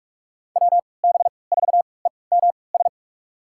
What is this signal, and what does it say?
Morse code, heard as a keyed tone, about 42 wpm, spaced as for 25 wpm, 700 Hz. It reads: WB4EMS
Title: USA Call Signs - 6 Characters - 3x Character Spacing - 45wpm